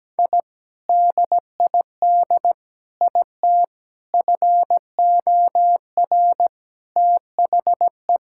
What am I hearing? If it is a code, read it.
Morse code, heard as a keyed tone, about 17 wpm, 700 Hz. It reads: I DID IT FOR THE
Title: Sentences from Top 1000 Words - Review Part 6 - 17wpm